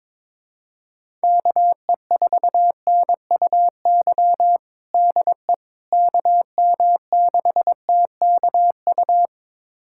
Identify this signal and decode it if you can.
Morse code, heard as a keyed tone, about 22 wpm, 700 Hz. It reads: KE4NUY DE KM6TKU
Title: USA Callsign DE Callsign 22wpm